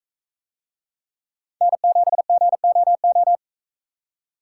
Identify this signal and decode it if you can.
Morse code, heard as a keyed tone, about 44 wpm, 700 Hz. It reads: N7GOO